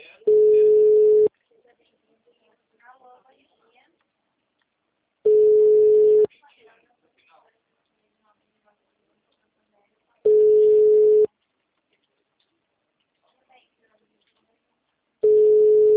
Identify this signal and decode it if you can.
Ringback tone (Europe, 425 Hz), with speech.